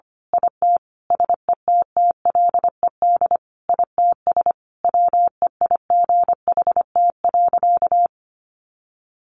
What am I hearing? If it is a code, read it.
Morse code, heard as a keyed tone, about 25 wpm, 700 Hz. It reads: IT SETTLED ITS WEIGHT.
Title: A Sound Of Thunder 25wpm